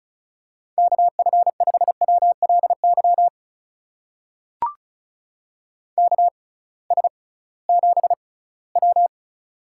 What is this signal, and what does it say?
Morse code, heard as a keyed tone, about 36 wpm, spaced as for 21 wpm, 700 Hz. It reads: KF5WLY KS7W
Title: USA Call Signs - 6x Character Spacing - 35wpm